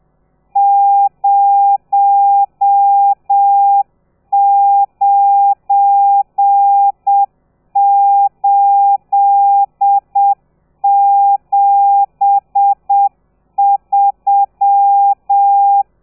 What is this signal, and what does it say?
Morse code, heard as a keyed tone, about 7 wpm, 800 Hz. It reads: 09873
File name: telegrafia / i2tfj / Lez12 30